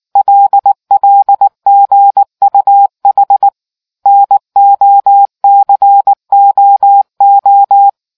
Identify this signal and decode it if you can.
Morse code, heard as a keyed tone, about 19 wpm, 800 Hz. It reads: LLGUH NOCOO